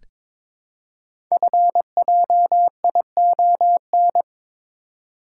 Morse code, heard as a keyed tone, about 22 wpm, 700 Hz. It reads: FJION
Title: Code Groups - 5 Letters - 22wpm